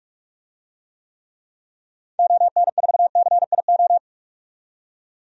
Morse code, heard as a keyed tone, about 46 wpm, 700 Hz. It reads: ON4CIO
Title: DXCC-100 Call Signs - 6 Characters - 45wpm